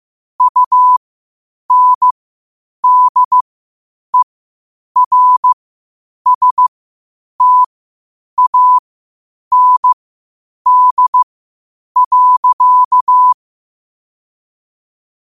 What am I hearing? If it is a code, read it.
Morse code, heard as a keyed tone, about 15 wpm, spaced as for 8 wpm, 1000 Hz. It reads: UNDERSTAND.
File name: qotdfiles / 8wpm